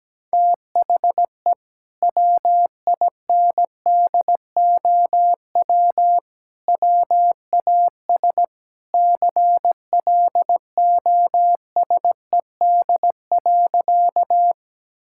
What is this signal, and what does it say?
Morse code, heard as a keyed tone, about 17 wpm, 700 Hz. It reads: THE WINDOW WAS CLOSED.
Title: Sentences from Top 1600 Words 17wpm